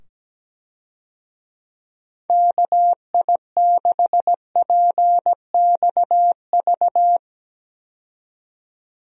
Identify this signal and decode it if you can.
Morse code, heard as a keyed tone, about 17 wpm, 700 Hz. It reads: KI6PXV
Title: USA Call Signs - 6 Characters - 17wpm